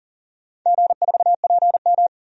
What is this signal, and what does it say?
Morse code, heard as a keyed tone, about 40 wpm, 700 Hz. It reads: G4PM
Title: DXCC-100 Call Signs - 4 Characters - 40wpm